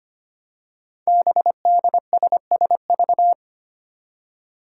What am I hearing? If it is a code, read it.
Morse code, heard as a keyed tone, about 25 wpm, 700 Hz. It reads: BDSSV
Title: Five Challenging Letter-Number Combinations 25wpm